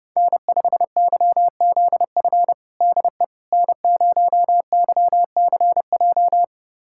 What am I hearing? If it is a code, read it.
Morse code, heard as a keyed tone, about 32 wpm, 700 Hz. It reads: N5YZF DE N0YCJ
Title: USA Callsign DE Callsign - 5 Characters - 30wpm